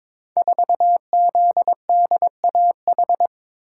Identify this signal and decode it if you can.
Morse code, heard as a keyed tone, about 22 wpm, 690 Hz. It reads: 4ZDAH
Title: Code Groups - 5 Letter:Numbers - 22wpm